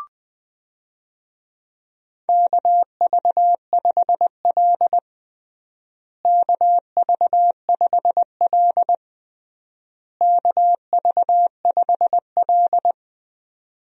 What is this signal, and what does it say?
Morse code, heard as a keyed tone, about 20 wpm, 700 Hz. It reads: KV5L KV5L KV5L
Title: USA Call Signs - 4 Characters - Repeated 3x - 20wpm